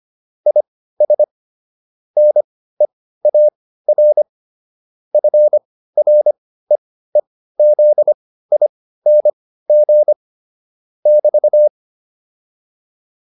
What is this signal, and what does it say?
Morse code, heard as a keyed tone, about 25 wpm, spaced as for 15 wpm, 600 Hz. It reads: IS NEAR FREEZING <BT>